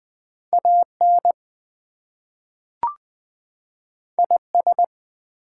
Morse code, heard as a keyed tone, about 20 wpm, 700 Hz. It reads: AN IS
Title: Two Letter Words - 20wpm